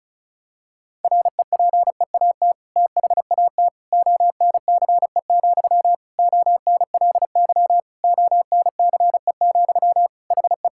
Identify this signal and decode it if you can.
Morse code, heard as a keyed tone, about 36 wpm, 700 Hz. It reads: REPEAT THAT ONCE, ONLY ONCE, HE
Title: Sentences from Top 1600 Words 35wpm